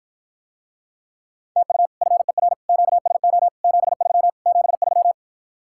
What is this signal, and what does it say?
Morse code, heard as a keyed tone, about 54 wpm, 700 Hz. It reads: TU FER QSO 73 73